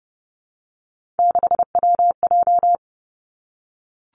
Morse code, heard as a keyed tone, about 30 wpm, 700 Hz. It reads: 6WJ
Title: Three Challenging Letter-Number Combinations 30wpm